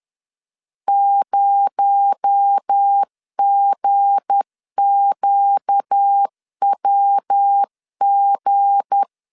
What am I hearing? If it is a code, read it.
Morse code, heard as a keyed tone, about 10 wpm, 800 Hz. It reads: 0GQWG
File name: LEZIONI / 7 GF9Z / 7 G 10wpm 2501